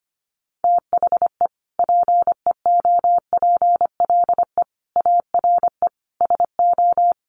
Morse code, heard as a keyed tone, about 25 wpm, 700 Hz. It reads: THE PEOPLE ARE SO
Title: Additional Easy Phrases 25wpm